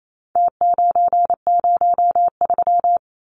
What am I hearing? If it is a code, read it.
Morse code, heard as a keyed tone, about 28 wpm, 700 Hz. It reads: T903
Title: Numbers - Contest Style - 4-Digit 28wpm